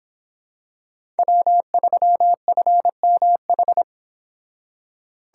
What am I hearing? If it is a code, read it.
Morse code, heard as a keyed tone, about 26 wpm, 700 Hz. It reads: W3FMH